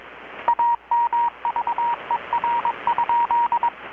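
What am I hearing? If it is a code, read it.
Morse code, heard as a keyed tone, about 21 wpm, 940 Hz, with radio noise. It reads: AMVER?
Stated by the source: Morse from KPH's Night of Nights XX maritime radiotelegraphy event, received in Chiapas, Mexico about 3,800 km away on a Kenwood R-600 with a grounded 10 m long-wire antenna.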